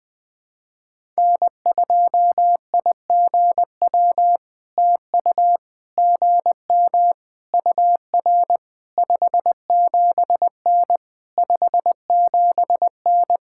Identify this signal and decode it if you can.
Morse code, heard as a keyed tone, about 20 wpm, 700 Hz. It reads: N2IGW TU GM UR 57N 57N
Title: POTA QSO Practice - 20wpm